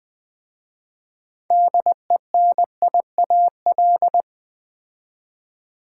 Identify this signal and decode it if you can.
Morse code, heard as a keyed tone, about 20 wpm, 700 Hz. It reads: DENIAL
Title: 6 Letter Words - First 3 Letters Given - 20wpm